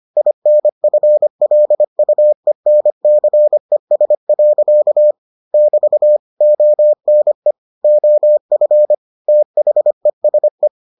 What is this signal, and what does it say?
Morse code, heard as a keyed tone, about 25 wpm, 600 Hz. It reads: INFLUENCES. <BT> ONE OF THESE